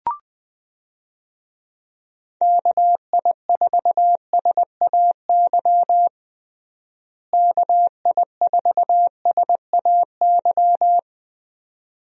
Morse code, heard as a keyed tone, about 20 wpm, 700 Hz. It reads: KI4SAY KI4SAY
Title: USA Call Signs - 6 Characters - Repeated 2x - 20wpm